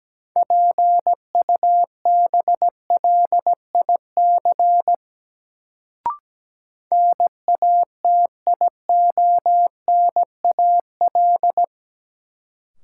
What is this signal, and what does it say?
Morse code, heard as a keyed tone, about 17 wpm, 700 Hz. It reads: PUBLIC NATIONAL
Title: Sets of 2 Words - New Word Drill - 17wpm